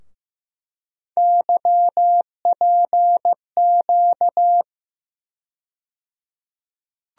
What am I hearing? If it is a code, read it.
Morse code, heard as a keyed tone, about 16 wpm, 700 Hz. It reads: YPQ
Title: USA Call Sign Postfixes - 3 Letters 15wpm